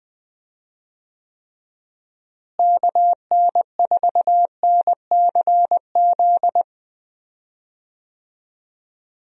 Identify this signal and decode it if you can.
Morse code, heard as a keyed tone, about 20 wpm, 700 Hz. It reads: KN4NCZ
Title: USA Call Signs - 6 Characters - Repeated 3x - 20wpm